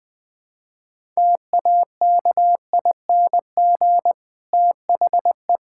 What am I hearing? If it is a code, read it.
Morse code, heard as a keyed tone, about 20 wpm, 700 Hz. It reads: TAKING THE
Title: Difficult Sentences with Hints - 20wpm